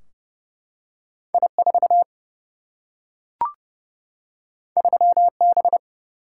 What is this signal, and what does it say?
Morse code, heard as a keyed tone, about 30 wpm, 700 Hz. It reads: I4 3B
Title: Single-Letter Single-Number 30wpm